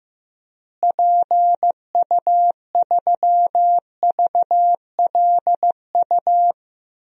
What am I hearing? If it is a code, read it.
Morse code, heard as a keyed tone, about 15 wpm, 700 Hz. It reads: PU3VLU
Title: DXCC-100 Call Signs - 6 Characters - Repeated 2x - 15wpm